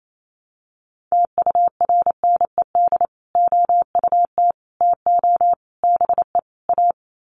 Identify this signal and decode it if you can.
Morse code, heard as a keyed tone, about 28 wpm, 700 Hz. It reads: TURNED OUT TO BE A
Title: Sets of 5 Words 28wpm